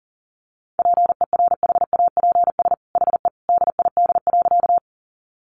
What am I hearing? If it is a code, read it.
Morse code, heard as a keyed tone, about 40 wpm, 700 Hz. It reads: PERHAPS HE DID.
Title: Sentences from Top 500 Words 40wpm